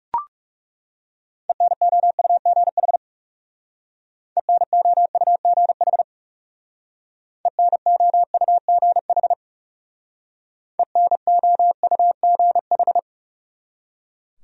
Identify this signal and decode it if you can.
Morse code, heard as a keyed tone, about 38 wpm, 700 Hz. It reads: ENOUGH ENOUGH ENOUGH ENOUGH